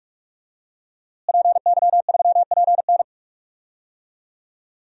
Morse code, heard as a keyed tone, about 46 wpm, 700 Hz. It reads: PY3PN
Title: DXCC-100 Call Signs - 5 Characters - Repeated 2x - 45wpm